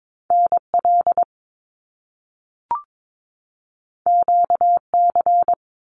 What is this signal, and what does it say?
Morse code, heard as a keyed tone, about 22 wpm, 700 Hz. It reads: NL QC
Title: Canadian Provinces and Territories 22wpm